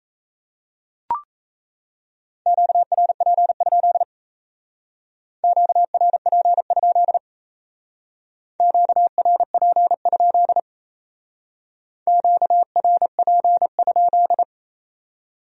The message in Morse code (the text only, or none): QRP? QRP? QRP? QRP?